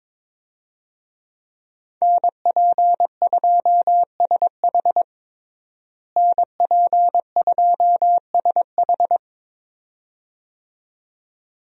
Morse code, heard as a keyed tone, about 22 wpm, 700 Hz. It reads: NP2SH NP2SH
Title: USA Call Signs - 5 Characters - Repeated 2x - 22wpm